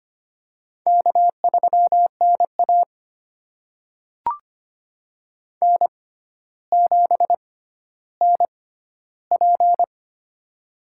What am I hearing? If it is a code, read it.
Morse code, heard as a keyed tone, about 25 wpm, spaced as for 13 wpm, 700 Hz. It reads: K3NA N7NP
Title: USA Call Signs - 6x Character Spacing - 25wpm